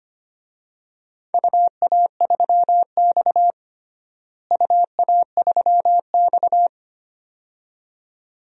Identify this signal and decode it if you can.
Morse code, heard as a keyed tone, about 25 wpm, 700 Hz. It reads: UA3X UA3X
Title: DXCC-100 Call Signs - 4 Characters - Repeated 2x - 25wpm